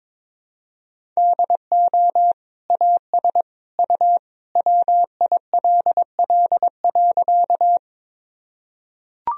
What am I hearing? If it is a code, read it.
Morse code, heard as a keyed tone, about 22 wpm, 700 Hz. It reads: DO AS U WILL.
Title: Sentences from Top 100 Words and Abbreviations 22wpm